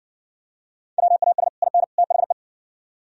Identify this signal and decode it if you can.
Morse code, heard as a keyed tone, about 58 wpm, 700 Hz. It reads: WAS IN THE